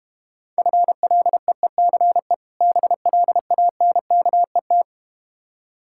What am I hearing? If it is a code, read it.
Morse code, heard as a keyed tone, about 31 wpm, 700 Hz. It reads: FLEECE BLANKET